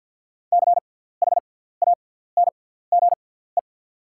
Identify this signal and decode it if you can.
Morse code, heard as a keyed tone, about 48 wpm, spaced as for 17 wpm, 690 Hz. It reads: CHANGE
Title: 6x Character Spacing - Top 200 Words - 50wpm